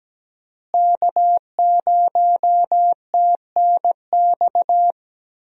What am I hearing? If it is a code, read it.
Morse code, heard as a keyed tone, about 17 wpm, 700 Hz. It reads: K0TNX
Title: USA Call Signs - 5 Characters - Repeated 3x - 17wpm